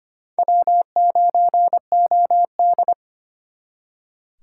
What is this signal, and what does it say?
Morse code, heard as a keyed tone, about 34 wpm, 700 Hz. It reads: W9OD